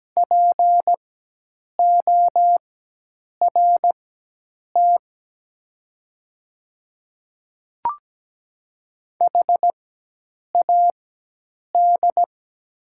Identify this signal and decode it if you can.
Morse code, heard as a keyed tone, about 17 wpm, spaced as for 8 wpm, 700 Hz. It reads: PORT HAD
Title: 4x Character Spacing - Top 300 Words - 17wpm